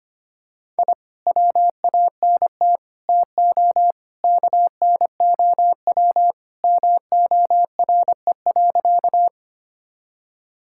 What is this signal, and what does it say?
Morse code, heard as a keyed tone, about 25 wpm, 700 Hz. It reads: I WANT TO KNOW MORE.
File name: practice / 25wpm / Sentences with 2-Word Phrases - 25wpm